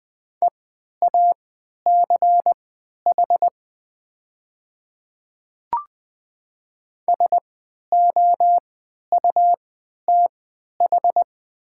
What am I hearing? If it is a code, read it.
Morse code, heard as a keyed tone, about 20 wpm, spaced as for 11 wpm, 700 Hz. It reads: EACH SOUTH
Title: 3x Character Spacing - Top 500 Words - 20wpm